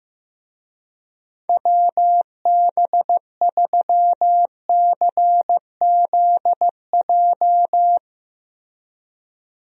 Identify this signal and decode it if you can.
Morse code, heard as a keyed tone, about 15 wpm, 700 Hz. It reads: WB3CZJ
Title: USA Call Signs - 6 Characters - Repeated 2x - 15wpm